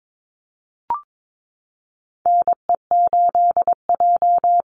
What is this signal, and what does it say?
Morse code, heard as a keyed tone, about 22 wpm, 690 Hz. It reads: NE8J
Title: Call Signs Repeated 3x - 22wpm